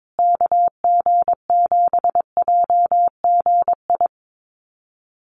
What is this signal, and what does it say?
Morse code, heard as a keyed tone, about 22 wpm, 700 Hz. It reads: KG7JGI
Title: USA Call Signs Repeated 3x 22wpm